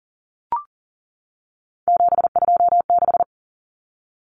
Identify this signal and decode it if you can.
Morse code, heard as a keyed tone, about 40 wpm, 700 Hz. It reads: 726
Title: Common Cut Numbers - 3-Digit 40wpm